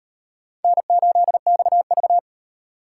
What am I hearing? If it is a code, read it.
Morse code, heard as a keyed tone, about 38 wpm, 700 Hz. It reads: N8XV